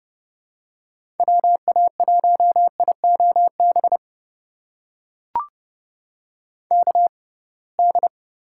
Morse code, heard as a keyed tone, about 30 wpm, spaced as for 23 wpm, 700 Hz. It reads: WA1IOB KD
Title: USA Call Signs - 6 Characters - 6x Character Spacing - 30wpm